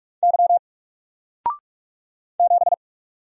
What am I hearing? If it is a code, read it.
Morse code, heard as a keyed tone, about 44 wpm, 700 Hz. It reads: Y 7
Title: Single Character (Letter:Number:Punctuation:Prosign) Alt 45wpm